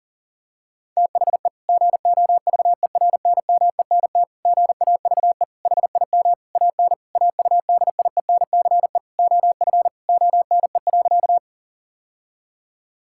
Morse code, heard as a keyed tone, about 40 wpm, 700 Hz. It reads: THE GOVERNMENT GAVE HIM AN AUDIENCE OF ONE.
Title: Sentences from Top 1000 Words - Review Part 5 - 40wpm